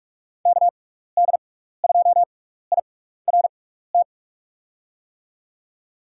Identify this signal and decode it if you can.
Morse code, heard as a keyed tone, about 44 wpm, spaced as for 15 wpm, 700 Hz. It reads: KD2IRT
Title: USA Call Signs - 6x Character Spacing - 45wpm